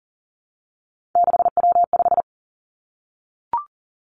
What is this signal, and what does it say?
Morse code, heard as a keyed tone, about 37 wpm, 700 Hz. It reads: 6W5
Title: Three Challenging Letter-Number Combinations 40wpm